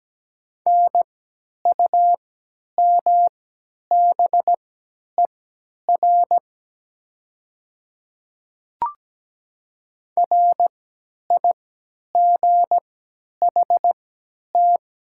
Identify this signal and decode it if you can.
Morse code, heard as a keyed tone, about 17 wpm, spaced as for 10 wpm, 690 Hz. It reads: NUMBER RIGHT
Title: 3x Character Spacing - Top 500 Words - 17wpm